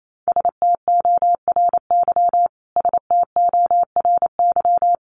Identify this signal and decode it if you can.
Morse code, heard as a keyed tone, about 28 wpm, 700 Hz. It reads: STORY STORY
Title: Top 500 Words Repeated 28wpm